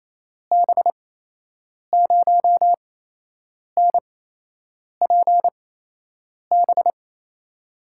Morse code, heard as a keyed tone, about 28 wpm, spaced as for 8 wpm, 700 Hz. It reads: B0NPB
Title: USA Call Signs - 6 Characters - 8x Character Spacing - 28wpm